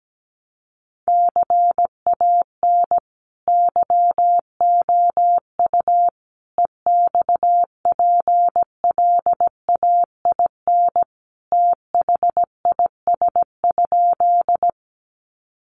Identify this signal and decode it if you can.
Morse code, heard as a keyed tone, about 17 wpm, 700 Hz. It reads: CAN YOU EXPLAIN THIS?
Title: Sentences from Top 1100 Words 17wpm